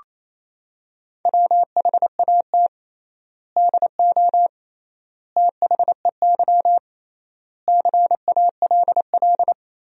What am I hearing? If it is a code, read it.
Morse code, heard as a keyed tone, about 27 wpm, 700 Hz. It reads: WHAT DO THEY CALL
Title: 3x Word-Spacing - Sentences from Top 100 Words - 28wpm